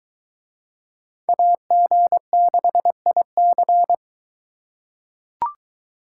Morse code, heard as a keyed tone, about 22 wpm, 700 Hz. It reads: AG6IC